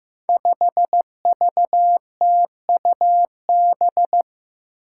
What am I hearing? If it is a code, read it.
Morse code, heard as a keyed tone, about 15 wpm, 700 Hz. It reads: 5VTUB
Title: Five Challenging Letter-Number Combinations 15wpm